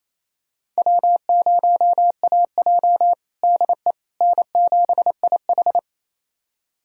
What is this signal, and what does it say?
Morse code, heard as a keyed tone, about 28 wpm, 700 Hz. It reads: W0AJ DE N7IH
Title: USA Callsign DE Callsign - 4 Characters - 28wpm